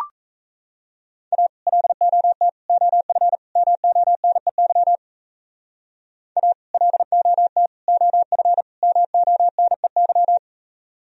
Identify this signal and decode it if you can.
Morse code, heard as a keyed tone, about 32 wpm, 700 Hz. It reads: A LOT OF MONEY A LOT OF MONEY
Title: Speed-Racing - Sets of 4 Words 28wpm